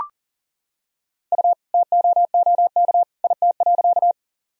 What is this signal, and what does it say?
Morse code, heard as a keyed tone, about 40 wpm, 700 Hz. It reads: U TOOK IT.
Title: Sentences from Top 300 Words and Abbreviations 40wpm